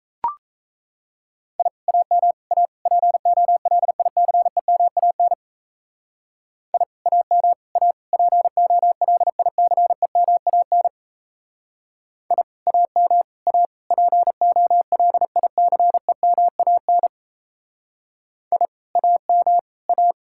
Morse code, heard as a keyed tone, about 37 wpm, 700 Hz. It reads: I AM A POLICEMAN I AM A POLICEMAN I AM A POLICEMAN I AM A